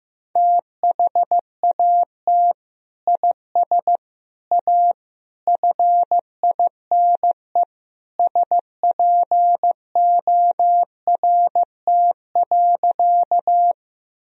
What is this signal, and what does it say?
Morse code, heard as a keyed tone, about 16 wpm, 700 Hz. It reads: THAT IS A FINE SPORT.
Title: Sentences from Top 1000 Words - Review Part 4 - 15wpm